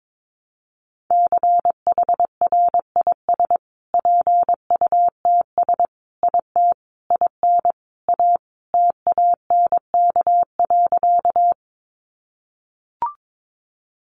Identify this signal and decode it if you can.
Morse code, heard as a keyed tone, about 22 wpm, 700 Hz. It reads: CHRIS PUTS IT IN A TANK.
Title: Additional Easy Phrases 22wpm